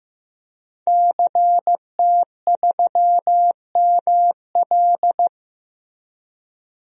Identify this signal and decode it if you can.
Morse code, heard as a keyed tone, about 15 wpm, 700 Hz. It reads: CT3ML
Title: DXCC-100 Call Signs - 5 Characters - Repeated 3x - 15wpm